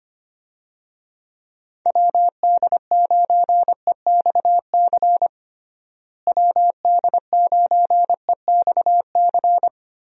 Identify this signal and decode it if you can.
Morse code, heard as a keyed tone, about 25 wpm, 700 Hz. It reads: WD9EXC WD9EXC
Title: USA Call Signs - 6 Characters - Repeated 3x - 25wpm